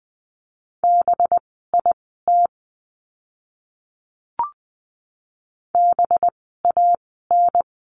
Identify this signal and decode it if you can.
Morse code, heard as a keyed tone, about 20 wpm, spaced as for 15 wpm, 700 Hz. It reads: BIT BAN
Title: 185 Lesson 30f - Letter b - Words 20wpm